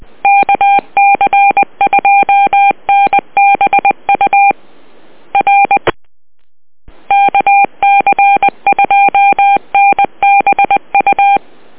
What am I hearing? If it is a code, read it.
Morse code, heard as a keyed tone, about 20 wpm, 800 Hz. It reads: KC2NBU R KC2NBU